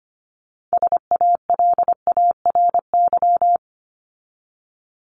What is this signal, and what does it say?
Morse code, heard as a keyed tone, about 25 wpm, 700 Hz. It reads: SALARY
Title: Top 1751-2000 Words 25wpm